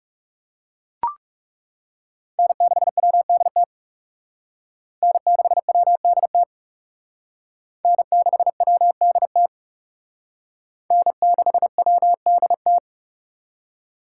Morse code, heard as a keyed tone, about 37 wpm, 700 Hz. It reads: N6WDT N6WDT N6WDT N6WDT
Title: Speed Racing - USA 1x3 Call Signs 30wpm